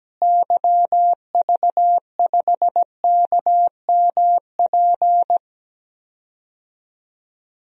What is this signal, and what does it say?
Morse code, heard as a keyed tone, about 17 wpm, 700 Hz. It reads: YV5KMP
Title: DXCC-100 Call Signs - 6 Characters - Repeated 2x - 17wpm